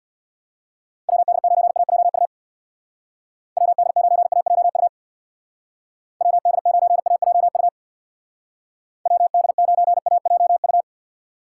Morse code, heard as a keyed tone, about 60 wpm, 700 Hz. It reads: WD9AJU WD9AJU WD9AJU WD9AJU
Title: Speed Racing - USA 2x3 Call Signs 50wpm